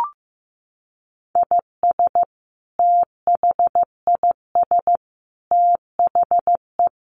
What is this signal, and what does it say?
Morse code, heard as a keyed tone, about 15 wpm, 700 Hz. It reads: IS THIS THE
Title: Sentences from Top 200 Words 15wpm